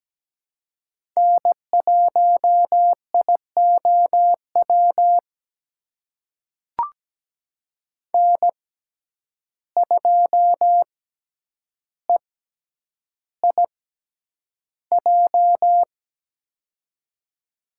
Morse code, heard as a keyed tone, about 17 wpm, spaced as for 9 wpm, 700 Hz. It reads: N1IOW N2EIJ
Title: USA Call Signs - 5 Characters - 6x Character Spacing - 17wpm